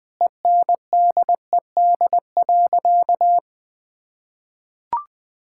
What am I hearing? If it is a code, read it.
Morse code, heard as a keyed tone, about 20 wpm, 700 Hz. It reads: ENDED.
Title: Sentences with 2-Word Phrases - 20wpm